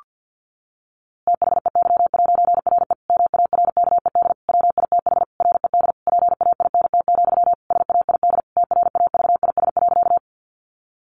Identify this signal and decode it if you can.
Morse code, heard as a keyed tone, about 50 wpm, 700 Hz. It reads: THEY'RE MARKED WITH RED PAINT, SAID TRAVIS.
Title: A Sound Of Thunder 50wpm